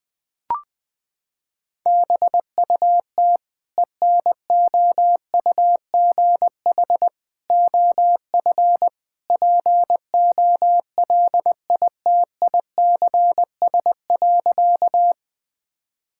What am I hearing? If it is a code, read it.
Morse code, heard as a keyed tone, about 20 wpm, 700 Hz. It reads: BUT ENOUGH OF POLITICS.
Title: Sentences from Top 1000 Words - Review Part 6 - 20wpm